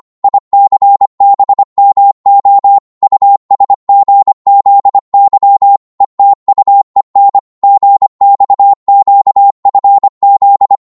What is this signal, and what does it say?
Morse code, heard as a keyed tone, about 25 wpm, 800 Hz. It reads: ICBMO USGZY ETUEN GXQFZ
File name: Morse / ltr-25wpm